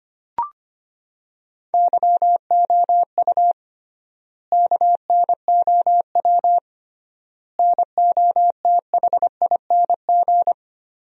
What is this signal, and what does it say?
Morse code, heard as a keyed tone, about 25 wpm, 700 Hz. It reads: YOU KNOW NOTHING
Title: 3x Word-Spacing - Sentences from Top 500 Words - 25wpm